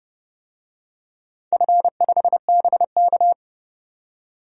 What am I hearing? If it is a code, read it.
Morse code, heard as a keyed tone, about 30 wpm, 700 Hz. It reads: F5BK
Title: DXCC-100 Call Signs - 4 Characters - 30wpm